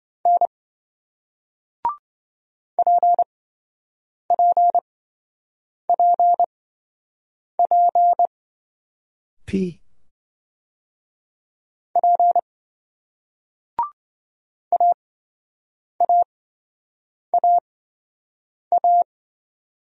Morse code, heard as a keyed tone, about 27 wpm, spaced as for 7 wpm, 700 Hz. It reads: N PPPP P AAAA